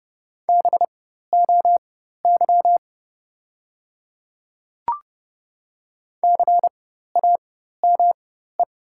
Morse code, heard as a keyed tone, about 31 wpm, spaced as for 14 wpm, 700 Hz. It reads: BOY CAME
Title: 4x Character Spacing - Top 200 Words - 30wpm